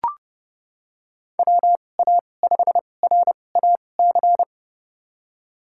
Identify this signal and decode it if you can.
Morse code, heard as a keyed tone, about 32 wpm, spaced as for 22 wpm, 700 Hz. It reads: WA5RAC
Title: USA Call Signs - 2x Character Spacing - 30wpm